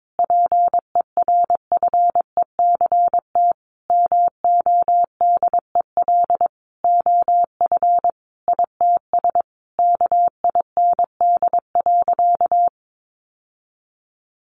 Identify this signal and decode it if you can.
Morse code, heard as a keyed tone, about 22 wpm, 700 Hz. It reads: PERFECT MODEL OF ITS KIND.
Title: Sentences from Top 1000 Words 22wpm